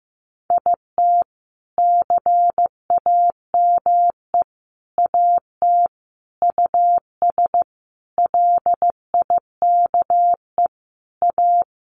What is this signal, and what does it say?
Morse code, heard as a keyed tone, about 15 wpm, 700 Hz. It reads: IT CAME AT US LIKE A
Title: Sentences from Top 1400 Words 15wpm